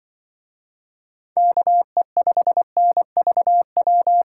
Morse code, heard as a keyed tone, about 24 wpm, 700 Hz. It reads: KE5NVW
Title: Speed Racing - USA 2x3 Call Signs 20wpm